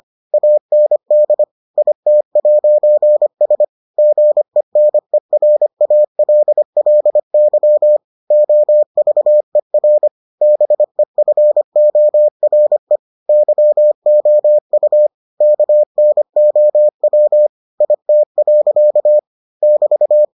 Morse code, heard as a keyed tone, about 25 wpm, 600 Hz. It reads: AND IT'S GENERALLY OVER BEFORE YOU KNOW IT. <BT>